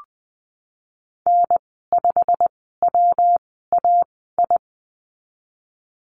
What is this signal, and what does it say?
Morse code, heard as a keyed tone, about 20 wpm, spaced as for 14 wpm, 700 Hz. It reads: N5WAI
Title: 186 Lesson 30g - Letter b - Call Signs 20wpm